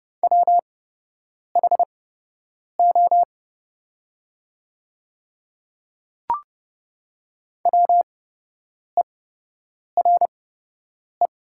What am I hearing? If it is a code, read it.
Morse code, heard as a keyed tone, about 31 wpm, spaced as for 8 wpm, 700 Hz. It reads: WHO WERE
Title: 8x Character Spacing - Top 100 Words - 30wpm